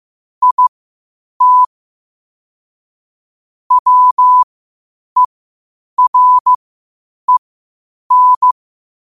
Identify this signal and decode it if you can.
Morse code, heard as a keyed tone, about 15 wpm, spaced as for 8 wpm, 1000 Hz. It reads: IT WEREN
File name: qotdfiles / 8wpm